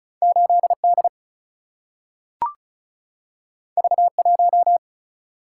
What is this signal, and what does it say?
Morse code, heard as a keyed tone, about 36 wpm, 700 Hz. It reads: 8D V1
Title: Single-Letter Single-Number 35wpm